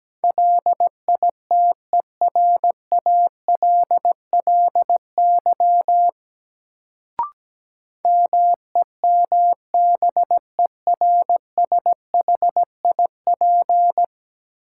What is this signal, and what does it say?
Morse code, heard as a keyed tone, about 17 wpm, 700 Hz. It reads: LITERALLY MEMBERSHIP
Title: Sentences from Top 1400 Words - New Word Drill Alt - 17wpm